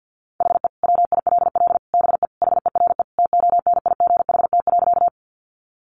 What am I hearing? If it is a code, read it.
Morse code, heard as a keyed tone, about 50 wpm, 700 Hz. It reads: HE WILL BE HERE TONIGHT.
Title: Sentences from Top 600 Words 50wpm